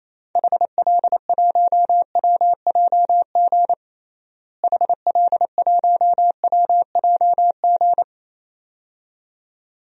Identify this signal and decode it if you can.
Morse code, heard as a keyed tone, about 28 wpm, 700 Hz. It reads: HL1WJG HL1WJG
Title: DXCC-100 Call Signs - 6 Characters - Repeated 2x - 28wpm